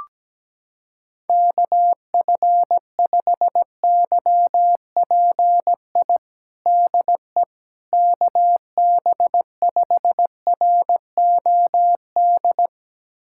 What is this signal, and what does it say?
Morse code, heard as a keyed tone, about 17 wpm, 700 Hz. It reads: KF5YPI DE KB5ROD
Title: USA Callsign DE Callsign 17wpm